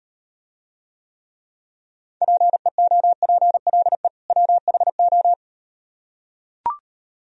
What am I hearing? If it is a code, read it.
Morse code, heard as a keyed tone, about 38 wpm, 700 Hz. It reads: PEOPLE WHO